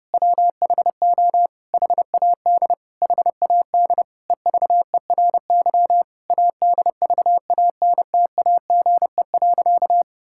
Morse code, heard as a keyed tone, about 30 wpm, 700 Hz. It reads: WHO HAD HAD EVERY ADVANTAGE.
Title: Sentences from Top 1000 Words - Review Part 6 - 30wpm